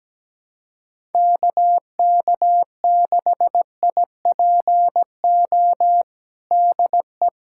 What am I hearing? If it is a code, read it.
Morse code, heard as a keyed tone, about 17 wpm, 700 Hz. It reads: KK6IPO DE
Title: USA Callsign DE Callsign 17wpm